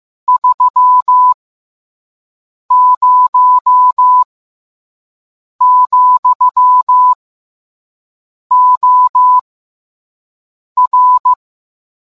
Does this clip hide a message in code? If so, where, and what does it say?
Yes: Morse code from 0.3 to 11.3 s, heard as a keyed tone, about 15 wpm, spaced as for 5 wpm, 1000 Hz. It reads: 30, OR